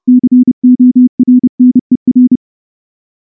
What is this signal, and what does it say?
Morse code, heard as a keyed tone, about 28 wpm, 250 Hz. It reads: CORNER
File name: morse / morse-training-30WPM